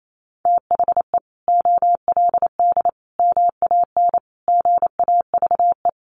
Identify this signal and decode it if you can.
Morse code, heard as a keyed tone, about 28 wpm, 700 Hz. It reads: THE OLD MAN GAVE
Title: Sentences from Top 500 Words 28wpm